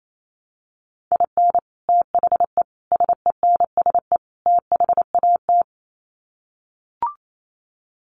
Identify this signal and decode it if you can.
Morse code, heard as a keyed tone, about 28 wpm, 700 Hz. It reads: IN THE SENSE THAT
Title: Sets of 4 Words 28wpm